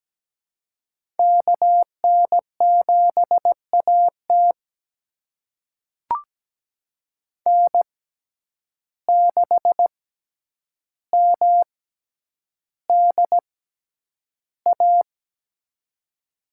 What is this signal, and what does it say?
Morse code, heard as a keyed tone, about 17 wpm, spaced as for 9 wpm, 700 Hz. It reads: KN7AT N6MDA
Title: USA Call Signs - 5 Characters - 6x Character Spacing - 17wpm